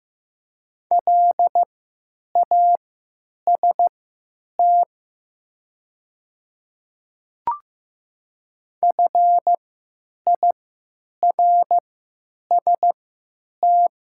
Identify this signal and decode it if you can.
Morse code, heard as a keyed tone, about 15 wpm, spaced as for 8 wpm, 700 Hz. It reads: LAST FIRST
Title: 3x Character Spacing - Top 300 Words - 15wpm